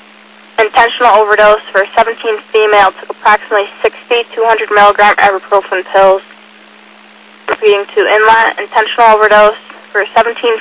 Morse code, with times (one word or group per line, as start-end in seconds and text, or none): none